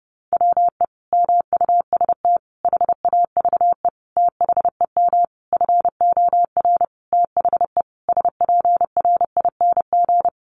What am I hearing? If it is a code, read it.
Morse code, heard as a keyed tone, about 30 wpm, 700 Hz. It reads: WE MUST HAVE THEM FOR THE SPRING
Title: Sentences from Top 1200 Words 30wpm